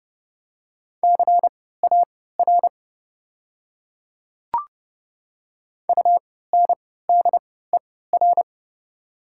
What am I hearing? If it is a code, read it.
Morse code, heard as a keyed tone, about 30 wpm, spaced as for 17 wpm, 700 Hz. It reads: CAR UNDER
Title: 3x Character Spacing - Top 300 Words - 30wpm